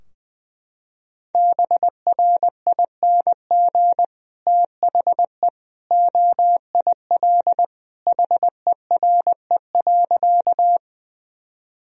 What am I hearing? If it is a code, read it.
Morse code, heard as a keyed tone, about 20 wpm, 700 Hz. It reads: BRING THE OIL HERE.
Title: Sentences from Top 1000 Words - Review Part 4 - 20wpm